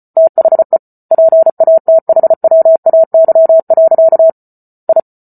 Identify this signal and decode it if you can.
Morse code, heard as a keyed tone, about 34 wpm, 650 Hz. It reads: THE PATHWAY. I